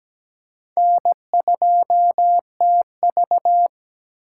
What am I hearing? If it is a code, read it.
Morse code, heard as a keyed tone, about 17 wpm, 700 Hz. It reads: N2TV
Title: USA Call Signs - 4 Characters - 4x Character Spacing - 17wpm